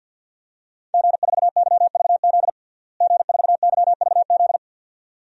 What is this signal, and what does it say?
Morse code, heard as a keyed tone, about 48 wpm, 700 Hz. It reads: G4YVZ G4YVZ
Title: DXCC-100 Call Signs - 5 Characters - Repeated 2x - 50wpm